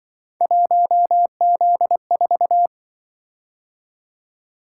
Morse code, heard as a keyed tone, about 24 wpm, 700 Hz. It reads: 1Z4
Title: Speed-Racing - Three Letter-Number Combinations - 20wpm